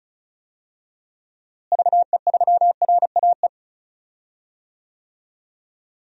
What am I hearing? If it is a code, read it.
Morse code, heard as a keyed tone, about 36 wpm, 700 Hz. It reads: VE3RAE
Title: DXCC-100 Call Signs - 6 Characters - Repeated 2x - 35wpm